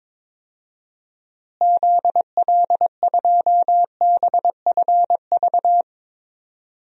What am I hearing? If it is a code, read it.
Morse code, heard as a keyed tone, about 22 wpm, 700 Hz. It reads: ZL2BFV
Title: DXCC-100 Call Signs - 6 Characters - Repeated 3x - 22wpm